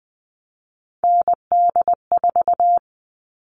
Morse code, heard as a keyed tone, about 20 wpm, 700 Hz. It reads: ND4
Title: All Cut Numbers - 3-Digit 20wpm